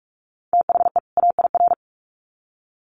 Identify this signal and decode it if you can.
Morse code, heard as a keyed tone, about 44 wpm, 700 Hz. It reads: THE AIR
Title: Sets of 2 Words 45wpm